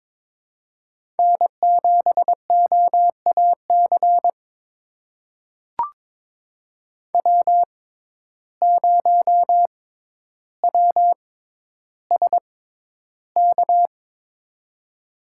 Morse code, heard as a keyed tone, about 22 wpm, spaced as for 11 wpm, 700 Hz. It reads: N7OAC W0WSK
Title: USA Call Signs - 6x Character Spacing - 22wpm